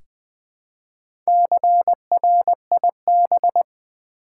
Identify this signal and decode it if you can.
Morse code, heard as a keyed tone, about 20 wpm, 700 Hz. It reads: CRIB
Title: Easy Words with Hints - 20wpm